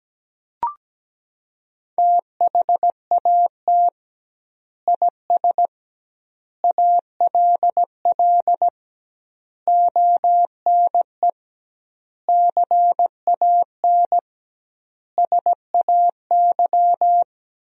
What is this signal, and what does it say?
Morse code, heard as a keyed tone, about 17 wpm, 700 Hz. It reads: THAT IS ALL ONE CAN SAY
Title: 2x Word-Spacing - Sentences from Top 200 Words - 17wpm